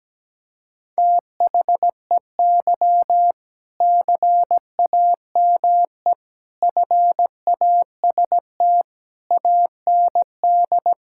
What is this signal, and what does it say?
Morse code, heard as a keyed tone, about 17 wpm, 700 Hz. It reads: THEY CAME FAST AND
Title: Sentences from Top 1600 Words 17wpm